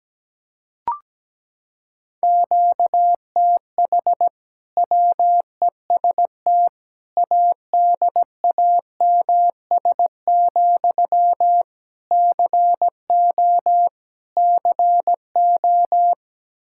Morse code, heard as a keyed tone, about 17 wpm, 700 Hz. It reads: QTH WEST ADAMS, CO CO
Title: Practice QSO - Exchange 1 - Beginning - 17wpm